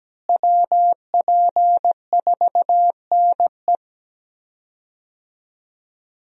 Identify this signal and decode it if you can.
Morse code, heard as a keyed tone, about 17 wpm, 700 Hz. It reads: WP4NE
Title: USA Call Signs - 5 Characters - Repeated 2x - 17wpm